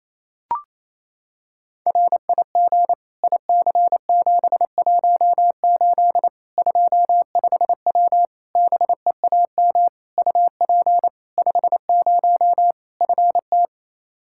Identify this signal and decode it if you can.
Morse code, heard as a keyed tone, about 28 wpm, 700 Hz. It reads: RIG IC718 25W BEAM UP 50 FT
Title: K1USN SST QSO Practice 28wpm